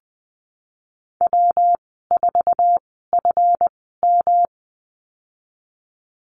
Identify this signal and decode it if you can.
Morse code, heard as a keyed tone, about 20 wpm, spaced as for 15 wpm, 700 Hz. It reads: W4FM
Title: 200 Lesson 32g - Letter k - Call Signs 20wpm